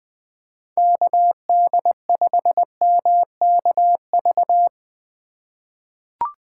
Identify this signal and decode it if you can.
Morse code, heard as a keyed tone, about 20 wpm, 700 Hz. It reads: KD5MKV
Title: USA Call Signs - 6 Characters - 2x Character Spacing - 20wpm